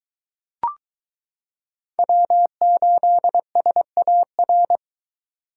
Morse code, heard as a keyed tone, about 22 wpm, 700 Hz. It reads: W8SAR